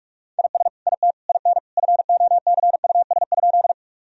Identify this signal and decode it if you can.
Morse code, heard as a keyed tone, about 48 wpm, 700 Hz. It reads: IS IT IN FOCUS?
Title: Sentences from Top 1000 Words - Review Part 4 - 45wpm